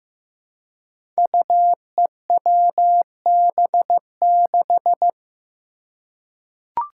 Morse code, heard as a keyed tone, about 15 wpm, 700 Hz. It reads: UEWB6
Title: Five Challenging Letter-Number Combinations 15wpm